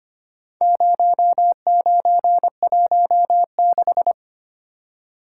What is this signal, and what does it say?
Morse code, heard as a keyed tone, about 25 wpm, 700 Hz. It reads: 0916
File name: practice / 25wpm / Four Letter-Number Combinations - 25wpm